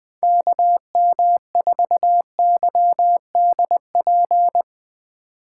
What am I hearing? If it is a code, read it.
Morse code, heard as a keyed tone, about 20 wpm, 700 Hz. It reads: KM4YDP
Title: USA Call Signs - 4x Character Spacing - 20wpm